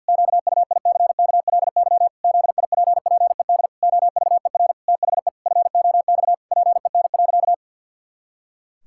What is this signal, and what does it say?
Morse code, heard as a keyed tone, about 50 wpm, 700 Hz. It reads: QUICKLY ZIPPED OVER THE FOX PEN.
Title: Pangram Challenge v2 - 50wpm